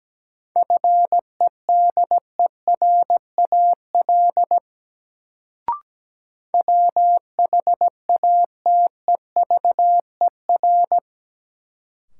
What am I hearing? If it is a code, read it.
Morse code, heard as a keyed tone, about 17 wpm, 700 Hz. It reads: FEDERAL WHATEVER
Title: Sentences from Top 1100 Words - New Word Drill Alt - 17wpm